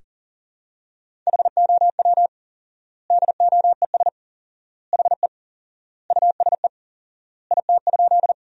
Morse code, heard as a keyed tone, about 40 wpm, 700 Hz. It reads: HOW DOES HE USE IT?
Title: 4x Word-Spacing - Sentences from Top 200 Words - 40wpm